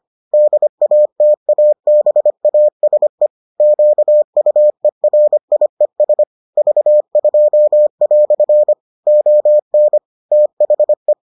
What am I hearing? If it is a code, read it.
Morse code, heard as a keyed tone, about 25 wpm, 600 Hz. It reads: DATABASE QUERIES V2" ON THE